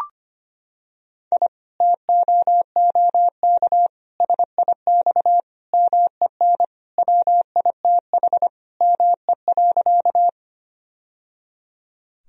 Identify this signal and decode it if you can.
Morse code, heard as a keyed tone, about 25 wpm, 700 Hz. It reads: I TOOK SIX MEN WITH ME.
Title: Sentences from Top 1000 Words - Review Part 2 - 25wpm